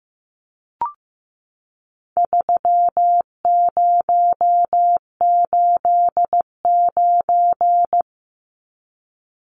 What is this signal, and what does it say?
Morse code, heard as a keyed tone, about 15 wpm, 700 Hz. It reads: 3089
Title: Numbers - 4-Digit 15wpm